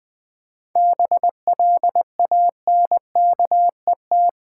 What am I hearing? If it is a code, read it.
Morse code, heard as a keyed tone, about 20 wpm, 700 Hz. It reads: BLANKET
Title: 7 Letter Words - First 3 Letters Given - 20wpm